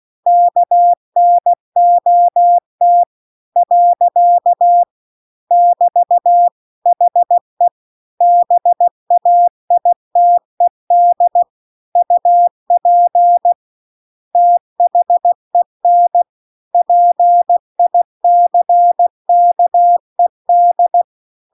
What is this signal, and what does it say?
Morse code, heard as a keyed tone, about 16 wpm, 700 Hz. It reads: KNOT . <BT> HE BAITED UP THEN PICKED